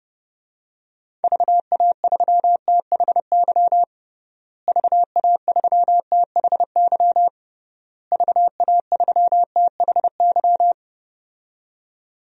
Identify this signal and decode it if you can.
Morse code, heard as a keyed tone, about 32 wpm, 700 Hz. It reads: VA3THY VA3THY VA3THY
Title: DXCC-100 Call Signs - 6 Characters - Repeated 3x - 30wpm